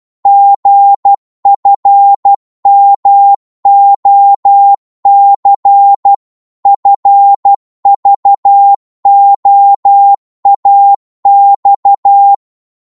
Morse code, heard as a keyed tone, about 12 wpm, 800 Hz. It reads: GFMOC FVOAX
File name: Morse / ltr-12wpm